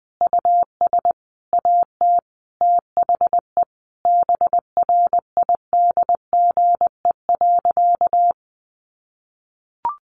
Morse code, heard as a keyed tone, about 20 wpm, 700 Hz. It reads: US AT THE BRIDGE.
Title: Sentences from Top 1200 Words 20wpm